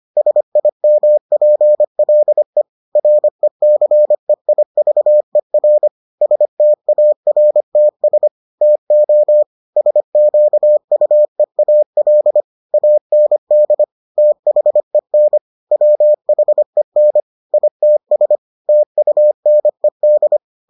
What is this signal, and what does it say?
Morse code, heard as a keyed tone, about 25 wpm, 600 Hz. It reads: SIMPLE RECEIVER STARTS TO SQUEAL AND THEN WHEN ITS TUNED